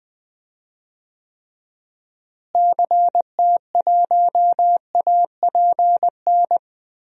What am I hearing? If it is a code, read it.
Morse code, heard as a keyed tone, about 20 wpm, 690 Hz. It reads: CT1APN
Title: DXCC-100 Call Signs - 6 Characters - Repeated 2x - 20wpm